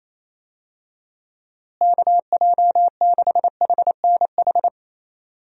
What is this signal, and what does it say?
Morse code, heard as a keyed tone, about 28 wpm, 700 Hz. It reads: KJ6HNH
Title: USA Call Signs - 6 Characters - Repeated 3x - 28wpm